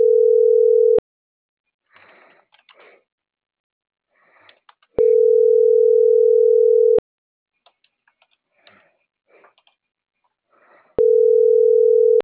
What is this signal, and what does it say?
Ringback tone (North America), 440 + 480 Hz.